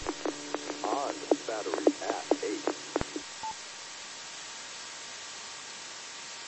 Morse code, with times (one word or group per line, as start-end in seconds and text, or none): none